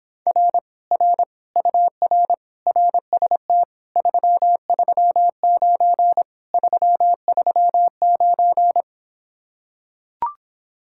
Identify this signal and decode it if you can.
Morse code, heard as a keyed tone, about 26 wpm, 700 Hz. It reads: R R UR RST 339 339